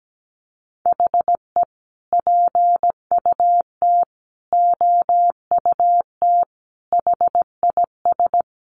Morse code, heard as a keyed tone, about 17 wpm, 700 Hz. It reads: HE PUT OUT HIS
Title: Sentences from Top 800 Words 17wpm